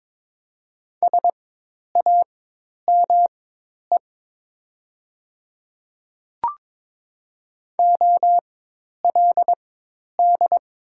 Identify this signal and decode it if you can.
Morse code, heard as a keyed tone, about 23 wpm, spaced as for 10 wpm, 700 Hz. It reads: SAME OLD
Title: 4x Character Spacing - Top 300 Words - 22wpm